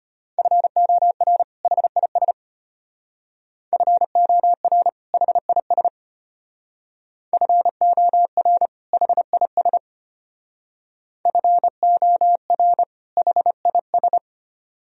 Morse code, heard as a keyed tone, about 32 wpm, 700 Hz. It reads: FOR HIS FOR HIS FOR HIS FOR HIS